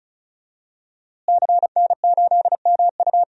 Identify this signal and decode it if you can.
Morse code, heard as a keyed tone, about 36 wpm, 700 Hz. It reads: CN8MU
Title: DXCC-100 Call Signs - 5 Characters - Repeated 3x - 35wpm